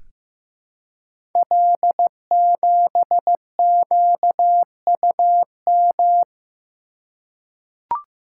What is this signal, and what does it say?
Morse code, heard as a keyed tone, about 15 wpm, 700 Hz. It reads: L7QUM
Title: Code Groups - 5 Letter:Numbers - 15wpm